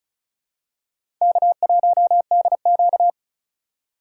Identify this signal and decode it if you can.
Morse code, heard as a keyed tone, about 36 wpm, 700 Hz. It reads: K1DQ
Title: USA Call Signs - 4 Characters - 8x Character Spacing - 35wpm